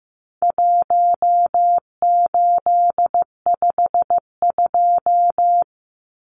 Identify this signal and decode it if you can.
Morse code, heard as a keyed tone, about 15 wpm, 700 Hz. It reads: 1852
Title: Numbers - 4-Digit 15wpm